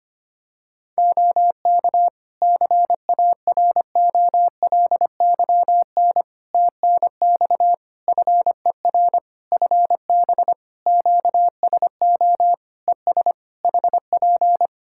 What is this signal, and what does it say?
Morse code, heard as a keyed tone, about 25 wpm, 700 Hz. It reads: OK CAROLYN TNX FER FB QSO ES HP
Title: Practice QSO - Exchange 4 - Ending - 25wpm